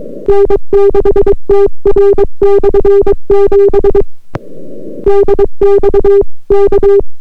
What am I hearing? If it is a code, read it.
Morse code, heard as a keyed tone, about 22 wpm, 400 Hz. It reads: N6TR/7 DX K